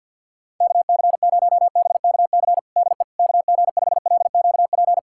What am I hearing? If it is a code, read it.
Morse code, heard as a keyed tone, about 50 wpm, 700 Hz. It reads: KC0BKC DE KG5LQP
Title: USA Callsign DE Callsign - 6 Characters - 50wpm